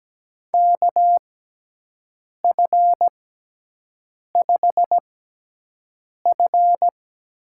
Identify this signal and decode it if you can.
Morse code, heard as a keyed tone, about 17 wpm, spaced as for 6 wpm, 700 Hz. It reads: KF5F
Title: USA Call Signs - 6 Characters - 6x Character Spacing - 17wpm